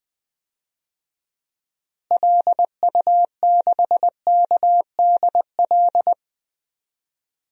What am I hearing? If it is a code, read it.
Morse code, heard as a keyed tone, about 20 wpm, 700 Hz. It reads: LU6KDL
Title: DXCC-100 Call Signs - 6 Characters - Repeated 3x - 20wpm